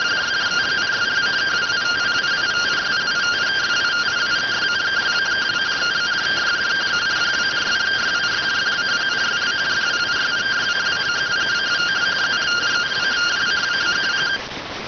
Two-tone teleprinter signal, about 170 Hz shift.